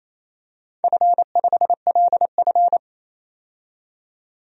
Morse code, heard as a keyed tone, about 28 wpm, 700 Hz. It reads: F5LF
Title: DXCC-100 Call Signs - 4 Characters - 28wpm